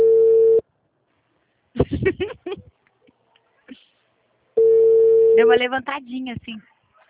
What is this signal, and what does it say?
Ringback tone (China, 450 Hz), with speech.